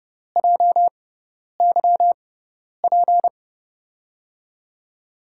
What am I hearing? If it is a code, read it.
Morse code, heard as a keyed tone, about 32 wpm, spaced as for 10 wpm, 700 Hz. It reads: JYP
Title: USA Call Signs - 6 Characters - 6x Character Spacing - 30wpm